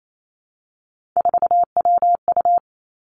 Morse code, heard as a keyed tone, about 28 wpm, 700 Hz. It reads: 4WU
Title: Three Challenging Letter-Number Combinations 28wpm